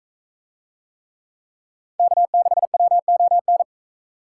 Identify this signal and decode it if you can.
Morse code, heard as a keyed tone, about 42 wpm, 700 Hz. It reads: K6WON